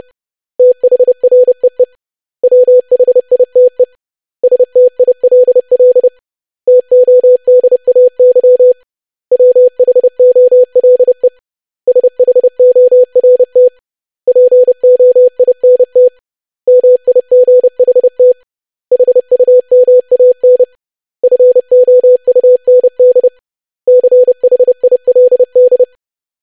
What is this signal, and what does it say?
Morse code, heard as a keyed tone, about 30 wpm, 500 Hz. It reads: THREE WHITE STILL TODAY WHOLE SHORT POINT MIGHT HUMAN FOUND CHILD
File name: lesson / web / basic500-25wpm